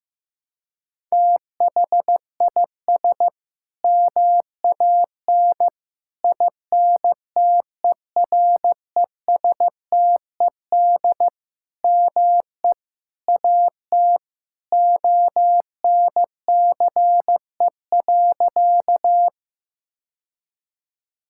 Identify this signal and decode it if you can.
Morse code, heard as a keyed tone, about 15 wpm, 700 Hz. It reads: THIS MAN INTERESTED ME AT ONCE.
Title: Sentences from Top 1000 Words - Review Part 6 - 15wpm